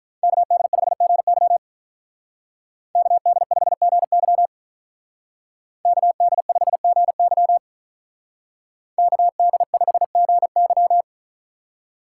Morse code, heard as a keyed tone, about 43 wpm, 700 Hz. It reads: KD5GY KD5GY KD5GY KD5GY
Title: Speed Racing - USA 2x2 Call Signs 35wpm